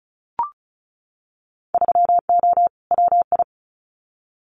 Morse code, heard as a keyed tone, about 34 wpm, 700 Hz. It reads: 3O WI
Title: Winter Field Day Exchanges 35wpm